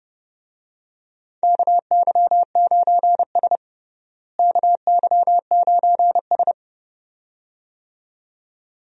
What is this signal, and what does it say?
Morse code, heard as a keyed tone, about 31 wpm, 700 Hz. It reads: KY9S KY9S
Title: USA Call Signs - 4 Characters - Repeated 2x - 30wpm